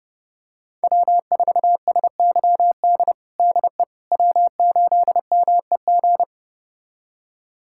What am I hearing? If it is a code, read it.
Morse code, heard as a keyed tone, about 32 wpm, 700 Hz. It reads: W4SYD DE W8MEG
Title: USA Callsign DE Callsign - 5 Characters - 30wpm